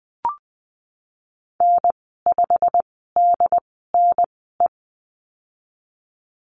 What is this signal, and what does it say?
Morse code, heard as a keyed tone, about 20 wpm, spaced as for 15 wpm, 700 Hz. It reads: N5DNE
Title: 186 Lesson 30g - Letter b - Call Signs 20wpm